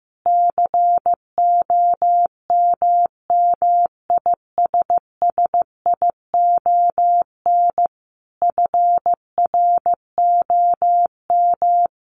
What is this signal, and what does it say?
Morse code, heard as a keyed tone, about 15 wpm, 700 Hz. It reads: COMMISSION FROM
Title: Sentences from Top 1400 Words 15wpm